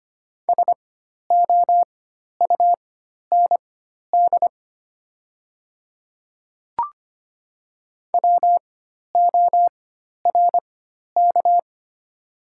Morse code, heard as a keyed tone, about 25 wpm, spaced as for 12 wpm, 690 Hz. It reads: SOUND WORK
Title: 4x Character Spacing - Top 300 Words - 25wpm